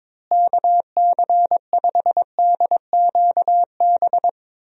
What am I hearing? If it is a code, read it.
Morse code, heard as a keyed tone, about 22 wpm, 700 Hz. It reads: KC5DQB
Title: USA Call Signs - 6 Characters - Repeated 3x - 22wpm